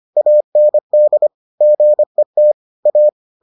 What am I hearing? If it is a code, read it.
Morse code, heard as a keyed tone, about 25 wpm, 600 Hz. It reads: AND GET A